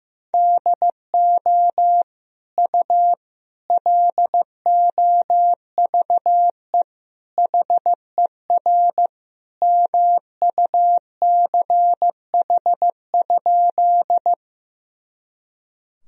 Morse code, heard as a keyed tone, about 15 wpm, 700 Hz. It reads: DO U LOVE HER MUCH?
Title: Sentences from Top 400 Words and Abbreviations 15wpm